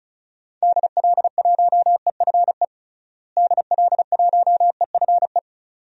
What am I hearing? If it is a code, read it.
Morse code, heard as a keyed tone, about 36 wpm, 700 Hz. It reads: DL1EFE DL1EFE
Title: DXCC-100 Call Signs - 6 Characters - Repeated 2x - 35wpm